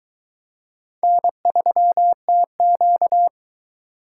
Morse code, heard as a keyed tone, about 22 wpm, 700 Hz. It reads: N3TQ